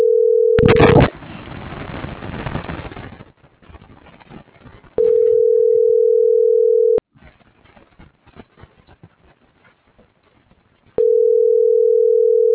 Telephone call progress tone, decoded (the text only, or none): ringback tone